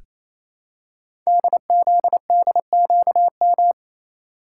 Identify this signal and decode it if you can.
Morse code, heard as a keyed tone, about 27 wpm, 700 Hz. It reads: DZDQM
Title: Code Groups - 5 Letters - 28wpm